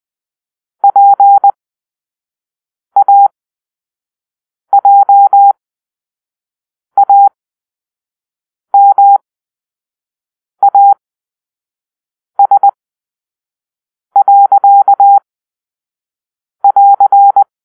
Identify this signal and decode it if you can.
Morse code, heard as a keyed tone, about 20 wpm, spaced as for 5 wpm, 800 Hz. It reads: PAJAMAS.<AR>
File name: arrl / Grp-4 Drill-4